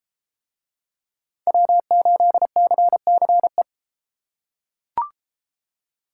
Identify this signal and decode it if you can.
Morse code, heard as a keyed tone, about 33 wpm, 700 Hz. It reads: W8CCE